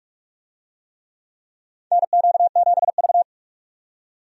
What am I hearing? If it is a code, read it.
Morse code, heard as a keyed tone, about 46 wpm, 700 Hz. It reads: NQ7V
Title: USA Call Signs - 4 Characters - 45wpm